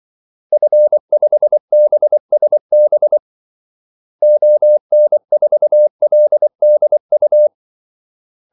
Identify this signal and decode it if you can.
Morse code, heard as a keyed tone, about 24 wpm, 600 Hz. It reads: F5BSB ON4LDU
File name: Callsigns-24wpm